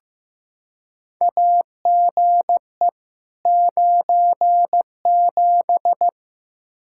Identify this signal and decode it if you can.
Morse code, heard as a keyed tone, about 15 wpm, 700 Hz. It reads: AGE 97
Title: Practice QSO - Exchange 3 - More Info - 15wpm